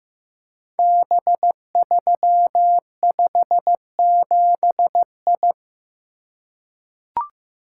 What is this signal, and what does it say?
Morse code, heard as a keyed tone, about 15 wpm, 700 Hz. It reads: B357I
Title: Code Groups - 5 Letter:Numbers - 15wpm